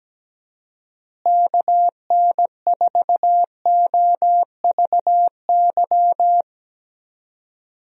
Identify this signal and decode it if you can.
Morse code, heard as a keyed tone, about 17 wpm, 700 Hz. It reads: KN4OVY
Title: USA Call Signs - 6 Characters - Repeated 3x - 17wpm